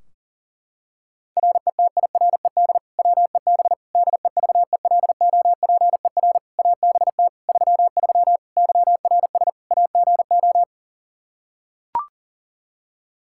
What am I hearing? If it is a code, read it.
Morse code, heard as a keyed tone, about 40 wpm, 700 Hz. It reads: RETIRED WEB DEVELOPER ABT 33 YRS AGO
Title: K1USN SST QSO Practice 40wpm